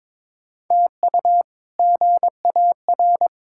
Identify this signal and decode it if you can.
Morse code, heard as a keyed tone, about 22 wpm, 700 Hz. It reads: TU GAR
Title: K1USN SST QSO Practice - 22wpm